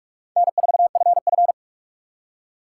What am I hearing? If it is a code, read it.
Morse code, heard as a keyed tone, about 62 wpm, spaced as for 53 wpm, 700 Hz. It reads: N4FF
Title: Speed Racing - USA 1x2 Call Signs 45wpm